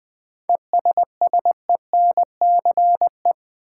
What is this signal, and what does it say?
Morse code, heard as a keyed tone, about 20 wpm, 700 Hz. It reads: ESSENCE
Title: 7 Letter Words - First 3 Letters Given - 20wpm